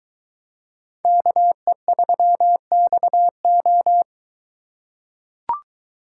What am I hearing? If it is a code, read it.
Morse code, heard as a keyed tone, about 21 wpm, 700 Hz. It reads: KE3XO